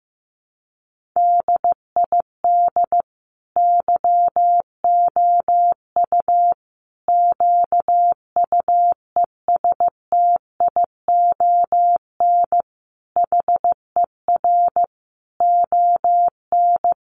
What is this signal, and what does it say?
Morse code, heard as a keyed tone, about 15 wpm, 700 Hz. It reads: DID YOU QUESTION HER ON
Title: Sentences from Top 800 Words 15wpm